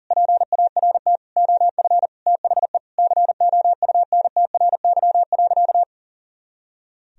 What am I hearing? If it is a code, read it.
Morse code, heard as a keyed tone, about 40 wpm, 700 Hz. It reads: PART OF THE COUNTRY.
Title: Sentences from Top 300 Words and Abbreviations 40wpm